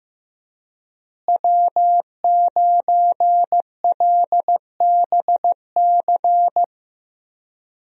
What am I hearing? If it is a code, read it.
Morse code, heard as a keyed tone, about 15 wpm, 700 Hz. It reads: W9LBC
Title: USA Call Signs - 5 Characters - Repeated 3x - 15wpm